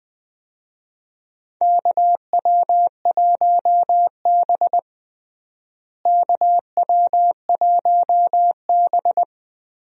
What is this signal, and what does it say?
Morse code, heard as a keyed tone, about 20 wpm, 700 Hz. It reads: KW1B KW1B
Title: USA Call Signs - 4 Characters - Repeated 3x - 20wpm